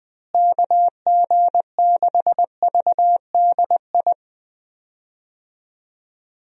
Morse code, heard as a keyed tone, about 21 wpm, 700 Hz. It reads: KG6VDI
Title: USA Call Signs - 6 Characters - Repeated 3x - 20wpm